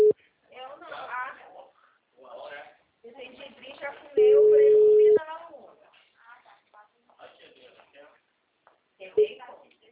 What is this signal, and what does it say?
Ringback tone (Europe, 425 Hz), with speech.